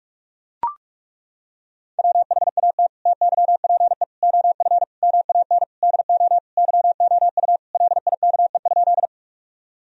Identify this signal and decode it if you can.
Morse code, heard as a keyed tone, about 41 wpm, 700 Hz. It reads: WHAT TYPE OF MAN DO YOU LIKE?
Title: Sentences from Top 1000 Words - Review Part 3 - 45wpm